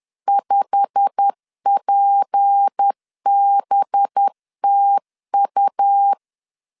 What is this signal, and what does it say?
Morse code, heard as a keyed tone, about 10 wpm, 800 Hz. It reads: 5PBTU
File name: LEZIONI / 8 YJ45 / 8 YJ45 10wpm 2549